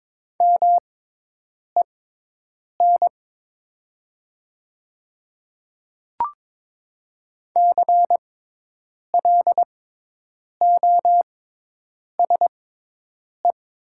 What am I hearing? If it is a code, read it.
Morse code, heard as a keyed tone, about 22 wpm, spaced as for 8 wpm, 700 Hz. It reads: MEN CLOSE
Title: 6x Character Spacing - Top 300 Words - 22wpm